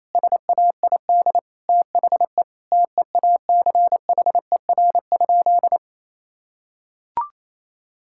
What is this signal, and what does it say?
Morse code, heard as a keyed tone, about 28 wpm, 700 Hz. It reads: SAID THE TEACHER?
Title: Sentences from Top 1000 Words - Review Part 6 - 28wpm